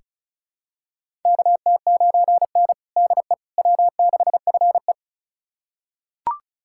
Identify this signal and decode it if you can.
Morse code, heard as a keyed tone, about 36 wpm, 700 Hz. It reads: KT9N DE W6FE
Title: USA Callsign DE Callsign - 4 Characters - 35wpm